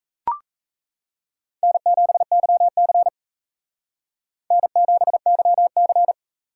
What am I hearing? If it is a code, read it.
Morse code, heard as a keyed tone, about 34 wpm, 700 Hz. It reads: N7YC N7YC
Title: Speed Racing - USA Call Signs 28wpm